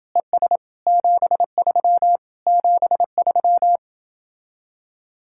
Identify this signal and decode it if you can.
Morse code, heard as a keyed tone, about 30 wpm, 700 Hz. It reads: ES 73 73